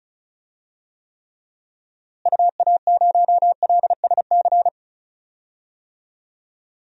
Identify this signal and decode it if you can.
Morse code, heard as a keyed tone, about 34 wpm, 700 Hz. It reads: UA0LSC
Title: DXCC-100 Call Signs - 6 Characters - Repeated 3x - 35wpm